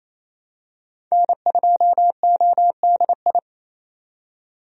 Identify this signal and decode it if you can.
Morse code, heard as a keyed tone, about 28 wpm, 700 Hz. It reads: N2ODI
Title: USA Call Signs - 5 Characters - 28wpm